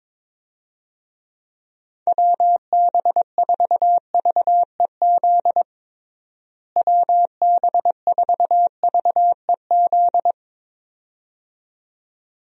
Morse code, heard as a keyed tone, about 22 wpm, 700 Hz. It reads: WB4VEZ WB4VEZ
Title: USA Call Signs - 6 Characters - Repeated 2x - 22wpm